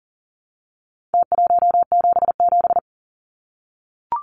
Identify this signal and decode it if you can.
Morse code, heard as a keyed tone, about 37 wpm, 700 Hz. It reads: T177
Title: Numbers - Contest Style - 4-Digit 40wpm